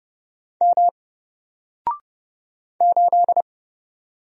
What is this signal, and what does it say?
Morse code, heard as a keyed tone, about 30 wpm, 700 Hz. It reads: M 8
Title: Single Character (Letter:Number:Punctuation:Prosign) Alt 30wpm